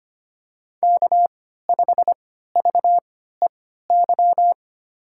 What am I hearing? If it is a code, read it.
Morse code, heard as a keyed tone, about 25 wpm, spaced as for 14 wpm, 700 Hz. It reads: K5VEY
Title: USA Call Signs - 5 Characters - 3x Character Spacing - 25wpm